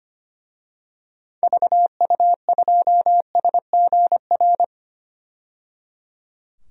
Morse code, heard as a keyed tone, about 25 wpm, 700 Hz. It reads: VU2SGR
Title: DXCC-100 Call Signs - 6 Characters - 25wpm